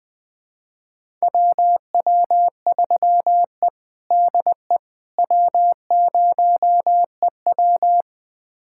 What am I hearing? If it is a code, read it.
Morse code, heard as a keyed tone, about 21 wpm, 700 Hz. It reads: WW3E DE W0EW
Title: USA Callsign DE Callsign - 4 Characters - 20wpm